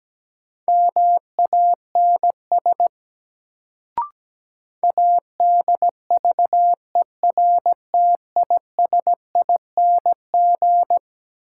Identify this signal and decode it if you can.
Morse code, heard as a keyed tone, about 17 wpm, 700 Hz. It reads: MANS ADVERTISING
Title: Sentences from Top 1000 Words - Review Part 6 - New Word Drill - 17wpm